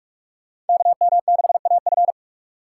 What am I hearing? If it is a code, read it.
Morse code, heard as a keyed tone, about 44 wpm, 700 Hz. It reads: KM6AF
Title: USA Call Signs - 5 Characters - 6x Character Spacing - 45wpm